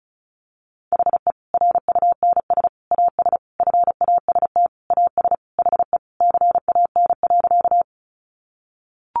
Morse code, heard as a keyed tone, about 34 wpm, 700 Hz. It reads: HE RUNS AS FAST AS HE CAN.
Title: Additional Easy Phrases 35wpm